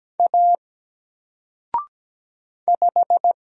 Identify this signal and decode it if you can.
Morse code, heard as a keyed tone, about 17 wpm, 700 Hz. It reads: A 5
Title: Single Character (Letter:Number:Punctuation:Prosign) 17wpm